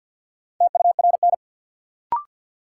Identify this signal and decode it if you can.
Morse code, heard as a keyed tone, about 53 wpm, 700 Hz. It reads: TURN
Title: Sets of 4 Words - New Word Drill - 50wpm